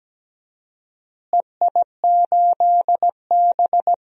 Morse code, heard as a keyed tone, about 17 wpm, 700 Hz. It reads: EI8B
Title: DXCC-100 Call Signs - 4 Characters - Repeated 3x - 17wpm